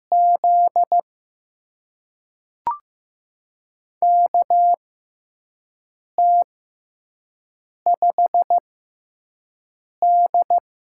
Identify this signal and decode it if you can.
Morse code, heard as a keyed tone, about 15 wpm, spaced as for 5 wpm, 700 Hz. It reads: Z KT5D
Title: USA Call Signs - 4 Characters - 6x Character Spacing - 15wpm